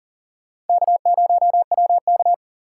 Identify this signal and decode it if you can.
Morse code, heard as a keyed tone, about 51 wpm, 700 Hz. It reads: K0WK